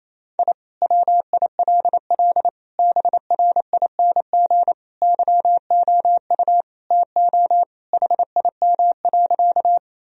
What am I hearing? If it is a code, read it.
Morse code, heard as a keyed tone, about 28 wpm, 700 Hz. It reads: I WILL BRING YOU TO HIM.
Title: Sentences from Top 1000 Words - Review Part 3 - 28wpm